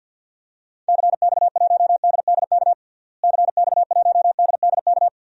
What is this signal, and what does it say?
Morse code, heard as a keyed tone, about 52 wpm, 700 Hz. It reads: CX1DDK CX1DDK
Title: DXCC-100 Call Signs - 6 Characters - Repeated 3x - 50wpm